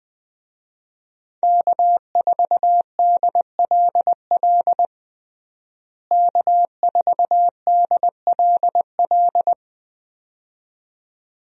Morse code, heard as a keyed tone, about 20 wpm, 700 Hz. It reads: K4DLL K4DLL
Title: USA Call Signs - 5 Characters - Repeated 2x - 20wpm